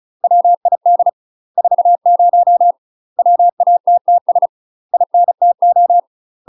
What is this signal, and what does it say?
Morse code, heard as a keyed tone, about 35 wpm, 700 Hz. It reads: WID 40 WATTS INTO
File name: UK MASTER QSOs TRAINING / 35WPM / UK MASTER QSOs 0004